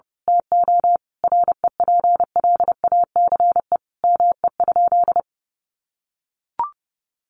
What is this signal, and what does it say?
Morse code, heard as a keyed tone, about 30 wpm, 700 Hz. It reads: TO REPLACE ME?
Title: Sentences from Top 1400 Words 30wpm